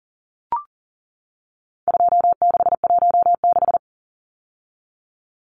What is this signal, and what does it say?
Morse code, heard as a keyed tone, about 40 wpm, 700 Hz. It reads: 2616
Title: Numbers - 4-Digit 40wpm